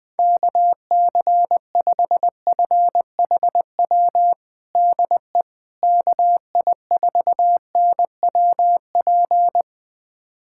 Morse code, heard as a keyed tone, about 20 wpm, 700 Hz. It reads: KC5FHW DE KI4NWP
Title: USA Callsign DE Callsign 20wpm